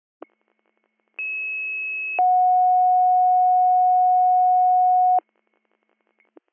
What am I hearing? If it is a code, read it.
Two-tone page, heard as a one-second tone, then a three-second tone, about 2600 then 730 Hz. It is heard from 1.2 to 5.2 s.